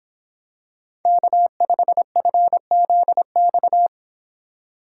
Morse code, heard as a keyed tone, about 26 wpm, 700 Hz. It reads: K5FZX